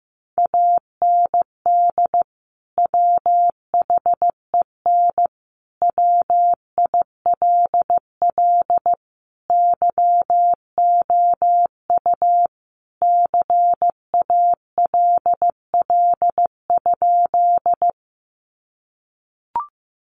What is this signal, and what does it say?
Morse code, heard as a keyed tone, about 15 wpm, 700 Hz. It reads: AND WHEN WILL YOU CALL?
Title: Sentences from Top 100 Words 15wpm